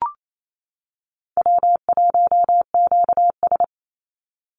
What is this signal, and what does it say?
Morse code, heard as a keyed tone, about 28 wpm, 700 Hz. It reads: W1QS
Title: Easy Call Signs Repeated 3x - 28wpm